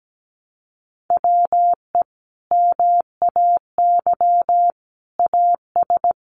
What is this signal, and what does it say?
Morse code, heard as a keyed tone, about 17 wpm, 700 Hz. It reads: WE MAY AS
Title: Sentences from Top 800 Words 17wpm